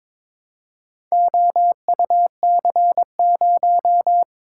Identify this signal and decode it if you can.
Morse code, heard as a keyed tone, about 22 wpm, 700 Hz. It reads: OUC0
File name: practice / 22wpm / Four Letter-Number Combinations - 22wpm